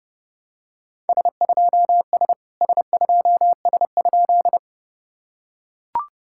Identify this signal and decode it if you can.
Morse code, heard as a keyed tone, about 30 wpm, 700 Hz. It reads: S2S S2S?
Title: SOTA QSO Practice 30wpm